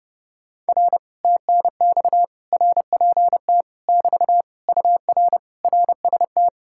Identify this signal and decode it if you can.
Morse code, heard as a keyed tone, about 32 wpm, 700 Hz. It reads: R TNX RPT <BT> UR RST